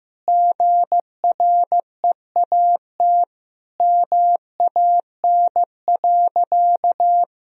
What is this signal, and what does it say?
Morse code, heard as a keyed tone, about 15 wpm, 700 Hz. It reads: GREAT MAN.
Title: Sentences from Top 1000 Words - Review Part 2 - 15wpm